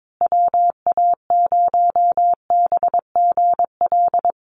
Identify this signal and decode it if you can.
Morse code, heard as a keyed tone, about 22 wpm, 700 Hz. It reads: WA0BGL
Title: USA Call Signs Repeated 3x 22wpm